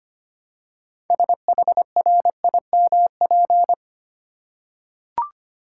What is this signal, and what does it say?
Morse code, heard as a keyed tone, about 25 wpm, 690 Hz. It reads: SHRIMP
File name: practice / 25wpm / Easy Words with Hints - 25wpm